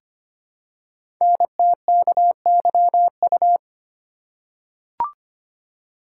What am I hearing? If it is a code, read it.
Morse code, heard as a keyed tone, about 25 wpm, 700 Hz. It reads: NTKYU
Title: Code Groups - 5 Letters - 25wpm